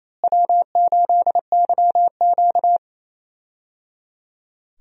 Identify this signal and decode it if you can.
Morse code, heard as a keyed tone, about 28 wpm, 700 Hz. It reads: W8YQ
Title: USA Call Signs - 4 Characters - Repeated 3x - 28wpm